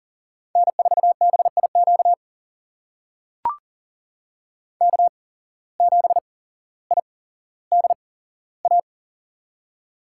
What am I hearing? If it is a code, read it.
Morse code, heard as a keyed tone, about 40 wpm, spaced as for 17 wpm, 700 Hz. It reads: N4BIQ K7IDA
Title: USA Call Signs - 5 Characters - 8x Character Spacing - 40wpm